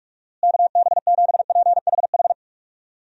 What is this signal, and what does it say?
Morse code, heard as a keyed tone, about 46 wpm, 700 Hz. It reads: KB7PHH
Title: USA Call Signs - 6 Characters - Repeated 3x - 45wpm